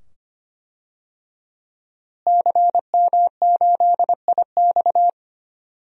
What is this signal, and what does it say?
Morse code, heard as a keyed tone, about 25 wpm, 700 Hz. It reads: CM8IX
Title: DXCC-100 Call Signs - 5 Characters - Repeated 2x - 25wpm